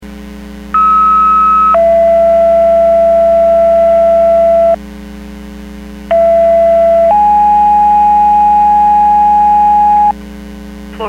Two-tone page, heard as a one-second tone, then a three-second tone, about 1290 then 670 Hz, about 670 then 830 Hz. It is heard from 0.7 to 10.1 s.
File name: Portage 154.385 Paging